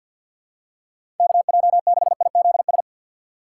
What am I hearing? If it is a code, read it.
Morse code, heard as a keyed tone, about 47 wpm, 700 Hz. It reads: KJ6IZS
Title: USA Call Signs - 6 Characters - 6x Character Spacing - 50wpm